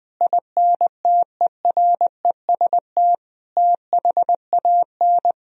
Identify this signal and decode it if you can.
Morse code, heard as a keyed tone, about 20 wpm, 700 Hz. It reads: INTEREST THAN
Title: Sentences from Top 1000 Words - Review Part 2 - 20wpm